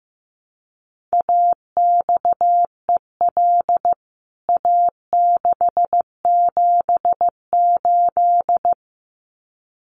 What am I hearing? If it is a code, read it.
Morse code, heard as a keyed tone, about 15 wpm, 700 Hz. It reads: AXEL A678
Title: CWT Exchange 15wpm